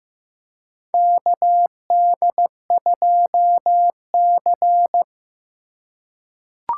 Morse code, heard as a keyed tone, about 15 wpm, 700 Hz. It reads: KD2C
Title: USA Call Signs - 4 Characters - 4x Character Spacing - 15wpm